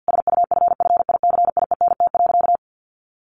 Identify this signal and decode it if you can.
Morse code, heard as a keyed tone, about 50 wpm, 700 Hz. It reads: SUFFICIENT.
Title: Sentences from Top 1400 Words 50wpm